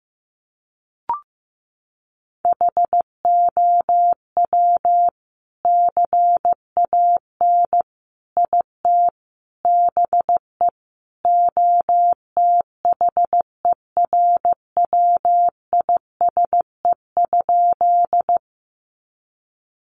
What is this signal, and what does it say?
Morse code, heard as a keyed tone, about 15 wpm, 700 Hz. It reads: HOW CAN IT BE OTHERWISE?
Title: Sentences from Top 1200 Words 15wpm